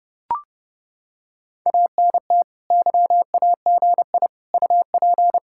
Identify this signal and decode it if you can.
Morse code, heard as a keyed tone, about 30 wpm, 700 Hz. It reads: ANT YAGI UP